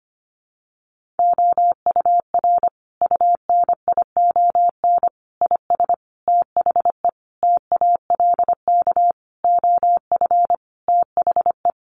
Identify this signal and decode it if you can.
Morse code, heard as a keyed tone, about 25 wpm, 700 Hz. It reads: OUR UNION IS THE TALK OF THE
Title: Sentences from Top 1400 Words 25wpm